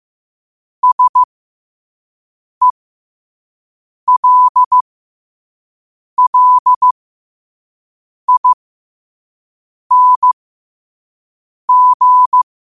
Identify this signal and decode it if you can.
Morse code, heard as a keyed tone, about 15 wpm, spaced as for 5 wpm, 1000 Hz. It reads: SELLING
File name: qotdfiles / 5wpm